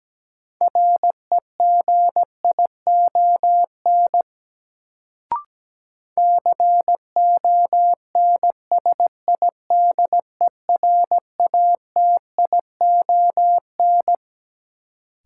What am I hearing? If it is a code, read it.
Morse code, heard as a keyed tone, about 17 wpm, 700 Hz. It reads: REGION CONSIDERATION
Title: Sentences from Top 1500 Words - New Word Drill - 17wpm